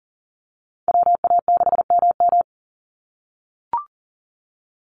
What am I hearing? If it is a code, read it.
Morse code, heard as a keyed tone, about 40 wpm, 700 Hz. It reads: WA6MM
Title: NA SOTA Activator Callsigns 40wpm